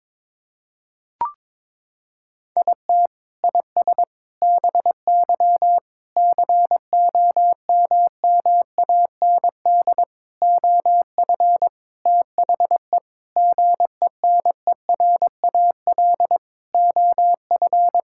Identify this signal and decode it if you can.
Morse code, heard as a keyed tone, about 21 wpm, 700 Hz. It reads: IT IS BY COMMAND OF THE GENERAL OF
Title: Sentences from Top 1600 Words 22wpm